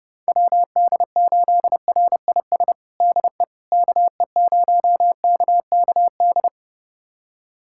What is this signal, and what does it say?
Morse code, heard as a keyed tone, about 30 wpm, 700 Hz. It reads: WD8RIS DE KE0KKD
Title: USA Callsign DE Callsign - 6 Characters - 30wpm